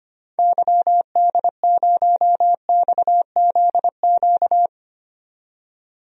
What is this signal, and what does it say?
Morse code, heard as a keyed tone, about 25 wpm, 700 Hz. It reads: YD0XZQ
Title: DXCC-100 Call Signs - 6 Characters - 25wpm